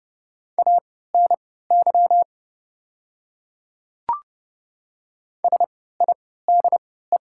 Morse code, heard as a keyed tone, about 30 wpm, spaced as for 17 wpm, 700 Hz. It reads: ANY SIDE
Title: 3x Character Spacing - Top 300 Words - 30wpm